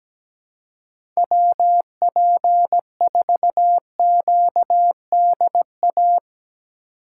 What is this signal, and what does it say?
Morse code, heard as a keyed tone, about 17 wpm, 700 Hz. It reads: WP4QDA
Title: USA Call Signs - 6 Characters - 3x Character Spacing - 17wpm